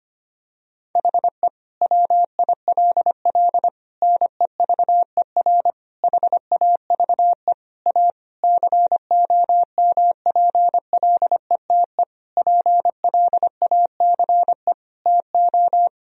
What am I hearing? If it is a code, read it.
Morse code, heard as a keyed tone, about 25 wpm, 700 Hz. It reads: HE WILL NEVER HAVE A COMPLETE PLACE TO
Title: Sentences from Top 1000 Words - Review Part 2 - 25wpm